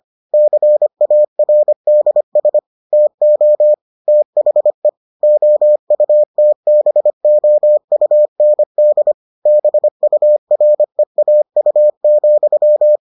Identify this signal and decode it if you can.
Morse code, heard as a keyed tone, about 25 wpm, 600 Hz. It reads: CARDS TO THE OUTBOUND BUREAU,